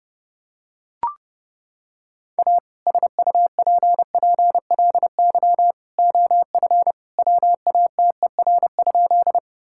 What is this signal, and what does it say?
Morse code, heard as a keyed tone, about 30 wpm, 700 Hz. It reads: A SUPPLY OF WATER?
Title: Sentences from Top 1000 Words - Review Part 5 - 30wpm